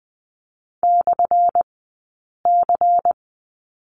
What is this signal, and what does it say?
Morse code, heard as a keyed tone, about 20 wpm, spaced as for 8 wpm, 700 Hz. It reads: /C
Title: 183 Lesson 30d - Letter b - Two Characters 20wpm